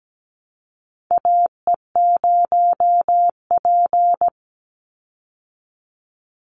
Morse code, heard as a keyed tone, about 17 wpm, 700 Hz. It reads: AE0P
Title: Call Signs Repeated 3x - 17wpm